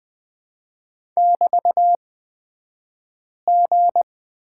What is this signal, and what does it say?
Morse code, heard as a keyed tone, about 20 wpm, spaced as for 5 wpm, 700 Hz. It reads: <BT>G
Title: 223 Lesson 36d - BT - Two Characters 20wpm